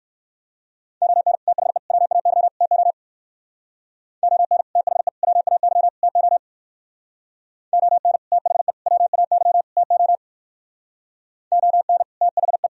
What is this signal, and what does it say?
Morse code, heard as a keyed tone, about 56 wpm, 700 Hz. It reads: ON THE WAY TO ON THE WAY TO ON THE WAY TO ON THE
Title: Speed-Racing - Sets of 4 Words 45wpm